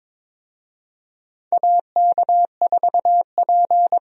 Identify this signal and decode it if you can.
Morse code, heard as a keyed tone, about 22 wpm, 700 Hz. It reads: AK4P
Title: USA Call Signs - 4 Characters - Repeated 2x - 22wpm